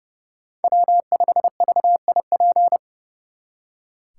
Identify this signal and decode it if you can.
Morse code, heard as a keyed tone, about 36 wpm, 700 Hz. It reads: W5VIP
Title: Speed Racing - USA 1x3 Call Signs 30wpm